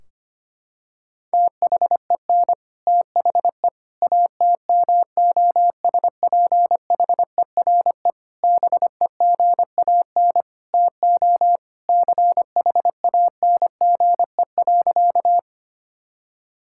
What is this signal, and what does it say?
Morse code, heard as a keyed tone, about 25 wpm, 700 Hz. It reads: THEN THE ATMOSPHERE BEGAN TO CHANGE.
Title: Sentences from Top 1600 Words 25wpm